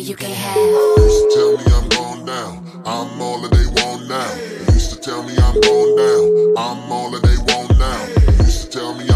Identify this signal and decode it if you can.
Ringback tone (Europe, 425 Hz), with noise.